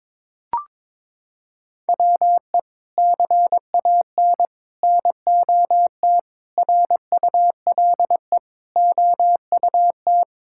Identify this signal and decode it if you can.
Morse code, heard as a keyed tone, about 22 wpm, 700 Hz. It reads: WE CAN NOT RULE OUT
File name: practice / 22wpm / Sets of 8 Words - 22wpm